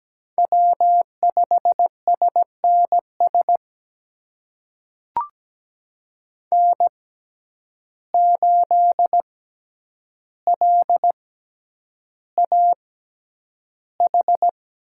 Morse code, heard as a keyed tone, about 17 wpm, spaced as for 9 wpm, 700 Hz. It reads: W5SNS N8LAH
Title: USA Call Signs - 5 Characters - 6x Character Spacing - 17wpm